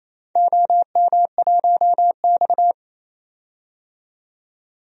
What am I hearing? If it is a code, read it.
Morse code, heard as a keyed tone, about 28 wpm, 700 Hz. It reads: OM1X
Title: DXCC-100 Call Signs - 4 Characters - 28wpm